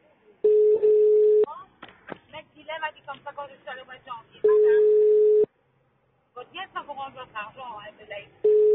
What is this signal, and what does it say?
Ringback tone (425 Hz, 1 s on, 3 s off), with speech.